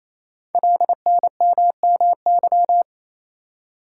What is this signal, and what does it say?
Morse code, heard as a keyed tone, about 27 wpm, 700 Hz. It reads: LNMMY
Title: Code Groups - 5 Letters - 28wpm